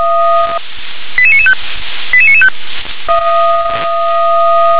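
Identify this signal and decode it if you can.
Five-tone selective call (ZVEI), heard as quick runs of five tones, tones of about 70 ms. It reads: 88995 88995
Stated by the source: ZVEI five-tone selective call 88993, received via SDR on channel 357.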